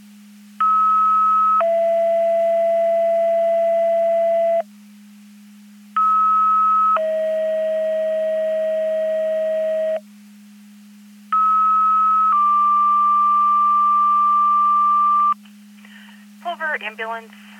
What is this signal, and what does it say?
Two-tone page, heard as a one-second tone, then a three-second tone, about 1290 then 670 Hz, about 1290 then 640 Hz, about 1290 then 1180 Hz, with speech. It is heard from 0.6 to 15.3 s.